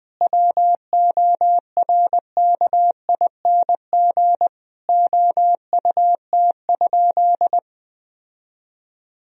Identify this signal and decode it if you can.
Morse code, heard as a keyed tone, about 20 wpm, 700 Hz. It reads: WORKING OUT?
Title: Sentences from Top 1000 Words - Review Part 4 - 20wpm